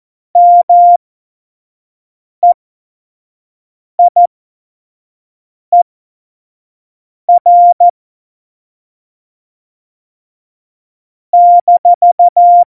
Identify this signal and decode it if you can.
Morse code, heard as a keyed tone, about 14 wpm, spaced as for 5 wpm, 700 Hz. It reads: MEIER -
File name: Morsekurs / Witze / Witz 01. September 2020, 25 F70